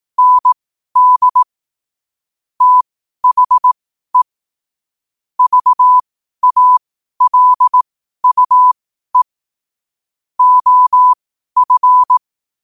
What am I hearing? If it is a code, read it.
Morse code, heard as a keyed tone, about 18 wpm, spaced as for 12 wpm, 1000 Hz. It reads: ND THE VALUE OF
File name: qotdfiles / 12wpm